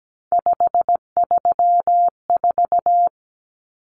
Morse code, heard as a keyed tone, about 17 wpm, 700 Hz. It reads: 534
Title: Common Cut Numbers - 3-Digit 17wpm